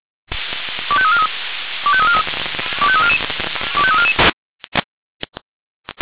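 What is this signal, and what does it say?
Five-tone selective call (ZVEI), heard as quick runs of five tones, tones of about 70 ms. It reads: 25343 25343 25344 25344
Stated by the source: Received on SDR, ZVEI five-tone call, address 25305.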